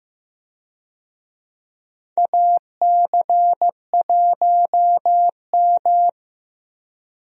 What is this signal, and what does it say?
Morse code, heard as a keyed tone, about 16 wpm, 700 Hz. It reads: AC1M
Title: USA Call Signs - 4 Characters - Repeated 3x - 15wpm